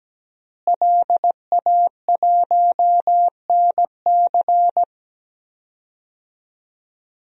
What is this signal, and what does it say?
Morse code, heard as a keyed tone, about 17 wpm, 700 Hz. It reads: LA1NC
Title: DXCC-100 Call Signs - 5 Characters - Repeated 2x - 17wpm